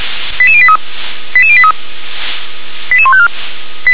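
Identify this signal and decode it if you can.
Five-tone selective call (ZVEI), heard as quick runs of five tones, tones of about 70 ms. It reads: 88083 88083 88154
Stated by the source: SDR-received ZVEI five-tone selective call, code 88863.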